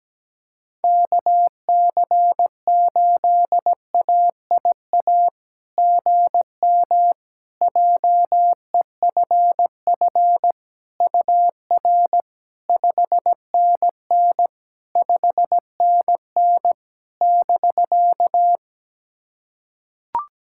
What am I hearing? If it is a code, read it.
Morse code, heard as a keyed tone, about 17 wpm, 700 Hz. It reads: KC8AIA GM JEFF UR 5NN 5NN <BK>
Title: SOTA QSO Practice 17wpm